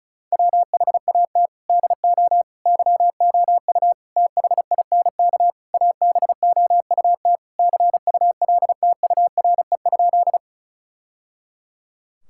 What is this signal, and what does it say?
Morse code, heard as a keyed tone, about 36 wpm, 700 Hz. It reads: WHAT DO YOU THINK ABOUT CULTURE?
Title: Sentences from Top 1000 Words - Review Part 6 - 35wpm